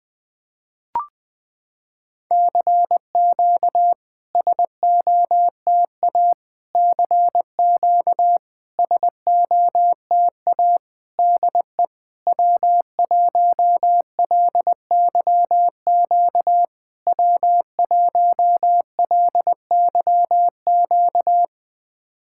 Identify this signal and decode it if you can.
Morse code, heard as a keyed tone, about 21 wpm, 700 Hz. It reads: CQ SOTA CQ SOTA DE W1LYQ W1LYQ
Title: SOTA QSO Practice 20wpm